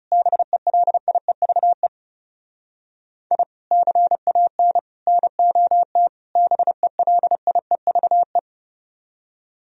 Morse code, heard as a keyed tone, about 40 wpm, 700 Hz. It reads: BELIEVE I CAN NOT BELIEVE